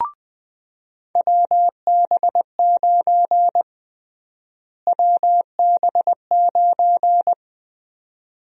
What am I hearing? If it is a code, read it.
Morse code, heard as a keyed tone, about 20 wpm, 700 Hz. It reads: WB9 WB9
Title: USA Call Sign Prefixes - Repeated 2x - 20wpm